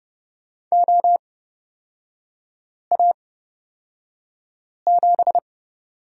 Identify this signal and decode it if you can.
Morse code, heard as a keyed tone, about 30 wpm, spaced as for 5 wpm, 700 Hz. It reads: OA7
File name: Lesson 37e - Letter x - Three Characters 30wpm